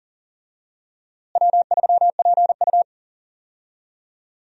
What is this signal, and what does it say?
Morse code, heard as a keyed tone, about 40 wpm, 700 Hz. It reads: W3PU
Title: USA Call Signs - 4 Characters - Repeated 2x - 40wpm